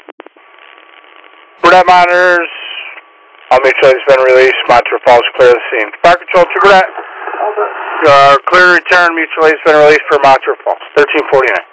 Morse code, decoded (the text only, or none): none